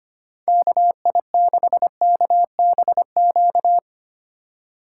Morse code, heard as a keyed tone, about 31 wpm, 700 Hz. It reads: KI6KBQ